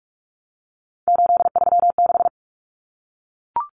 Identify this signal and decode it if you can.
Morse code, heard as a keyed tone, about 44 wpm, 700 Hz. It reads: 836
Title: Numbers - 3-Digit 45wpm